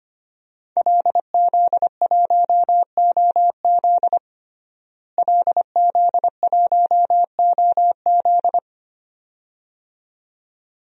Morse code, heard as a keyed tone, about 25 wpm, 700 Hz. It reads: LZ1OZ LZ1OZ
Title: DXCC-100 Call Signs - 5 Characters - Repeated 2x - 25wpm